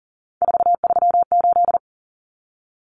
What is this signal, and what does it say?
Morse code, heard as a keyed tone, about 40 wpm, 700 Hz. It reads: 438
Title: Common Cut Numbers - 3-Digit 40wpm